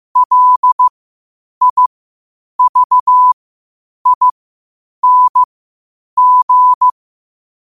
Morse code, heard as a keyed tone, about 15 wpm, spaced as for 8 wpm, 1000 Hz. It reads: LIVING
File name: qotdfiles / 8wpm